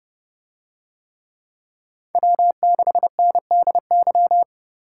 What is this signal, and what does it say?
Morse code, heard as a keyed tone, about 30 wpm, 700 Hz. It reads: W6NDY